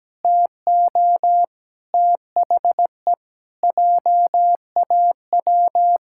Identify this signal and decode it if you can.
Morse code, heard as a keyed tone, about 17 wpm, 700 Hz. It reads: TO THE JAW
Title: Pangram Challenge v2 - 17wpm